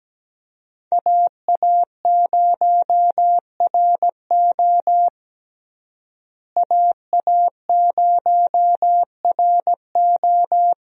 Morse code, heard as a keyed tone, about 17 wpm, 700 Hz. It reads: AA0RO AA0RO
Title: USA Call Signs - 5 Characters - Repeated 3x - 17wpm